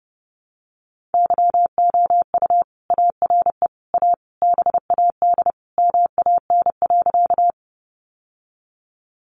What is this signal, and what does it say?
Morse code, heard as a keyed tone, about 30 wpm, 700 Hz. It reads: YOU ARE A BAD MAN.
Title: Sentences from Top 700 Words 30wpm